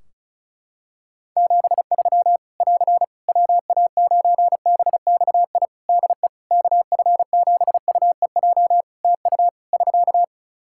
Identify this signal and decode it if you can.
Morse code, heard as a keyed tone, about 36 wpm, 700 Hz. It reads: 73 <AR> WA9BXI DE KF7UEJ TU <SK>
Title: K1USN SST QSO Practice 35wpm